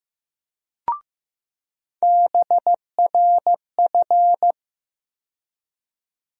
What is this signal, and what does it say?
Morse code, heard as a keyed tone, about 15 wpm, 700 Hz. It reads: BRF
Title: USA Call Sign Postfixes - 3 Letters 15wpm